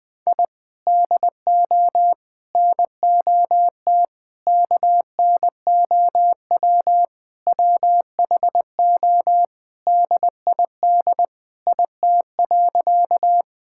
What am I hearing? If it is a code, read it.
Morse code, heard as a keyed tone, about 20 wpm, 700 Hz. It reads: I DO NOT KNOW WHO DID IT.
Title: Sentences with 5-Word Phrases - Spoken Beginning - 20wpm